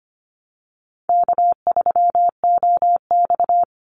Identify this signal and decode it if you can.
Morse code, heard as a keyed tone, about 25 wpm, 700 Hz. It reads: K3OX
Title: Easy Call Signs Repeated 2x - 25wpm